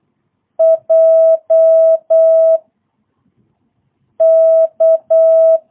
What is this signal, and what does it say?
Morse code, heard as a keyed tone, about 8 wpm, 650 Hz. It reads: J K
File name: jh8cbh / 2022cw